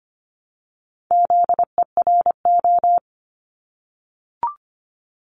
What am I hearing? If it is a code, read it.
Morse code, heard as a keyed tone, about 25 wpm, 700 Hz. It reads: ZERO
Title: Multi-letter Phonemes Repeated 3x 25wpm